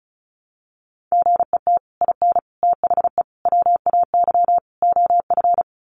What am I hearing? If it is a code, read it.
Morse code, heard as a keyed tone, about 34 wpm, 700 Hz. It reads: GET IN THE WAY OF
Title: Sets of 5 Words Encore 35wpm